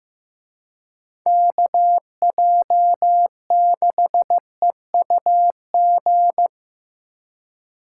Morse code, heard as a keyed tone, about 18 wpm, 700 Hz. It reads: KJ6EUG
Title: Speed Racing - USA 2x3 Call Signs 15wpm